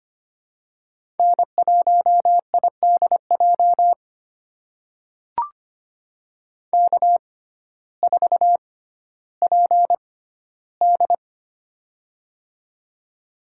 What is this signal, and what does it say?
Morse code, heard as a keyed tone, about 25 wpm, spaced as for 14 wpm, 700 Hz. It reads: N1IDJ K4PD
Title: USA Call Signs - 6x Character Spacing - 25wpm